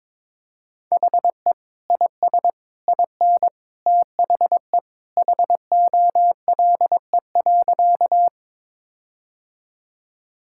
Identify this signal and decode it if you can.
Morse code, heard as a keyed tone, about 21 wpm, 700 Hz. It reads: HE IS IN THE HOLE.
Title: Sentences from Top 1600 Words 22wpm